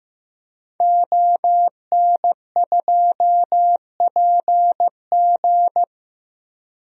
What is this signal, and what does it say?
Morse code, heard as a keyed tone, about 15 wpm, 700 Hz. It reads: ON2PG
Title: DXCC-100 Call Signs - 5 Characters - Repeated 3x - 15wpm